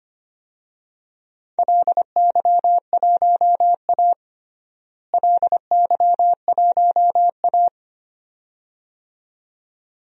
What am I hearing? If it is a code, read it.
Morse code, heard as a keyed tone, about 25 wpm, 700 Hz. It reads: LY1A LY1A
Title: DXCC-100 Call Signs - 4 Characters - Repeated 2x - 25wpm